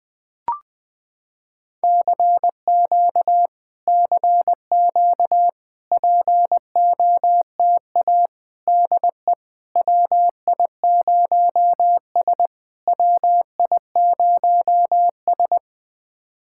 Morse code, heard as a keyed tone, about 20 wpm, 700 Hz. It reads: CQ CQ POTA DE WI0S WI0S
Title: POTA QSO Practice - 20wpm